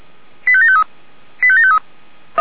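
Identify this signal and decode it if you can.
Five-tone selective call (ZVEI), heard as quick runs of five tones, tones of about 70 ms. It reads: 85753 85753
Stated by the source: ZVEI five-tone selective call 85722 received by SDR.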